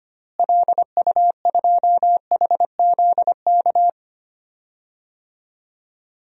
Morse code, heard as a keyed tone, about 25 wpm, 700 Hz. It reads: LU2HZK
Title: DXCC-100 Call Signs - 6 Characters - Repeated 3x - 25wpm